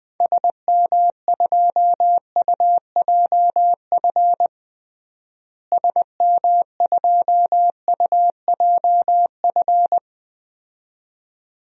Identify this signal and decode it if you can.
Morse code, heard as a keyed tone, about 20 wpm, 700 Hz. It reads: SM2UJF SM2UJF
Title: DXCC-100 Call Signs - 6 Characters - Repeated 2x - 20wpm